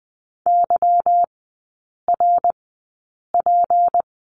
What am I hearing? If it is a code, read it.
Morse code, heard as a keyed tone, about 20 wpm, spaced as for 8 wpm, 700 Hz. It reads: YRP
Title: 152 Lesson 25e - Letter p - Three Characters 20wpm